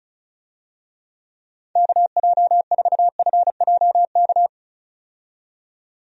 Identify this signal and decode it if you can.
Morse code, heard as a keyed tone, about 36 wpm, 700 Hz. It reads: KJ4FJK
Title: USA Call Signs - 6 Characters - 35wpm